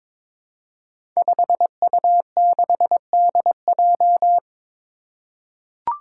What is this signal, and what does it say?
Morse code, heard as a keyed tone, about 22 wpm, 700 Hz. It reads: 5U6DJ
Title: Five Challenging Letter-Number Combinations 22wpm